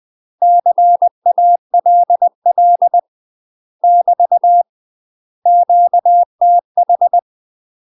Morse code, heard as a keyed tone, about 20 wpm, 700 Hz. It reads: CALL <BT> QTH
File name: UK MASTER QSOs TRAINING / 20WPM / UK MASTER QSOs 0034